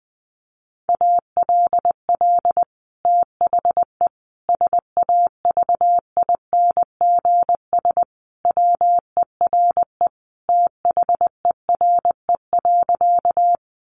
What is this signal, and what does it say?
Morse code, heard as a keyed tone, about 20 wpm, 700 Hz. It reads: ALL THE SAVINGS WERE THERE.
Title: Sentences from Top 1200 Words 20wpm